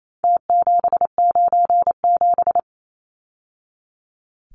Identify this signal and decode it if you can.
Morse code, heard as a keyed tone, about 28 wpm, 700 Hz. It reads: T797
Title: Numbers - Contest Style - 4-Digit 28wpm